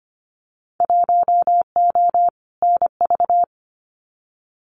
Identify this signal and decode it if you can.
Morse code, heard as a keyed tone, about 25 wpm, 700 Hz. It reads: 1O NV
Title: Winter Field Day Exchanges Encore 25wpm